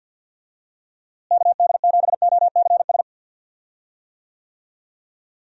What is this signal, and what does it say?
Morse code, heard as a keyed tone, about 52 wpm, 700 Hz. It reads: KD7OCS
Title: USA Call Signs - 6 Characters - Repeated 3x - 50wpm